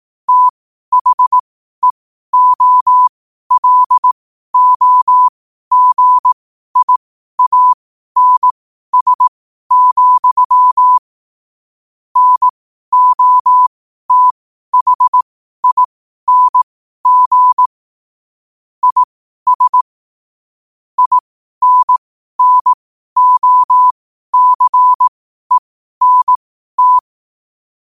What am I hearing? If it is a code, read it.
Morse code, heard as a keyed tone, about 18 wpm, spaced as for 12 wpm, 1000 Hz. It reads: THEOLOGIANS, NOTHING IS INNOCENT
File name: qotdfiles / 12wpm